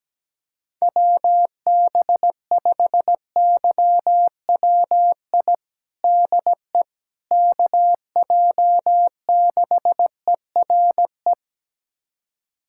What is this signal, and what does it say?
Morse code, heard as a keyed tone, about 17 wpm, 700 Hz. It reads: WB5YWI DE KJ6ERE
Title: USA Callsign DE Callsign - 6 Characters - 17wpm